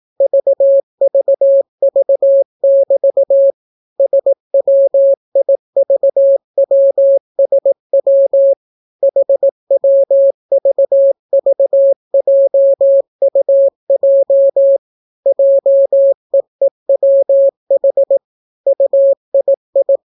Morse code, heard as a keyed tone, about 18 wpm, 550 Hz. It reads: VVV<BT> SWIVWSW HWVVJUJ JEEWH UII